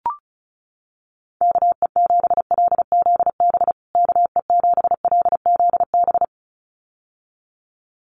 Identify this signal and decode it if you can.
Morse code, heard as a keyed tone, about 34 wpm, 700 Hz. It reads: KE7LZB KE7LZB
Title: USA Call Signs Repeated 2x 35wpm